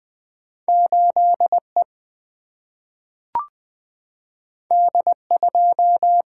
Morse code, heard as a keyed tone, about 20 wpm, 700 Hz. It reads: 8 E D 2
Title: Two Letter-Number Combinations - 20wpm